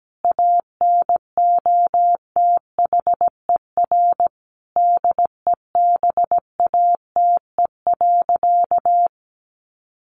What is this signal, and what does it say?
Morse code, heard as a keyed tone, about 17 wpm, 700 Hz. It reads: ANOTHER DEBATE.
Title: Sentences from Top 1100 Words 17wpm